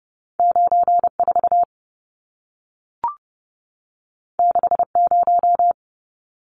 Morse code, heard as a keyed tone, about 30 wpm, 700 Hz. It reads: 94 60
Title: Numbers - 2-Digit 30wpm